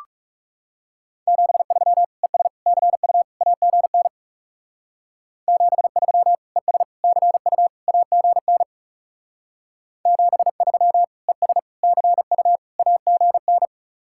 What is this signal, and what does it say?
Morse code, heard as a keyed tone, about 37 wpm, 700 Hz. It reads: 73 ES CU AGN 73 ES CU AGN 73 ES CU AGN
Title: Speed-Racing Common QSO Phrases Encore - 30wpm